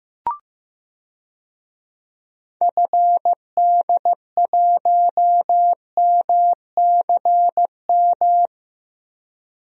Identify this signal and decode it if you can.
Morse code, heard as a keyed tone, about 15 wpm, 700 Hz. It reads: FD1MCM
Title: DXCC-100 Call Signs - 6 Characters - Repeated 3x - 15wpm